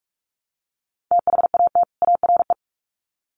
Morse code, heard as a keyed tone, about 44 wpm, 700 Hz. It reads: THAT ARE
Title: Sets of 2 Words 45wpm